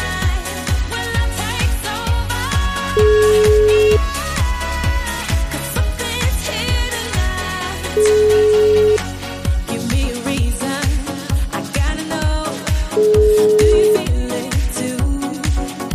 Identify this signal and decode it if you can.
Ringback tone (Europe, 425 Hz), with noise.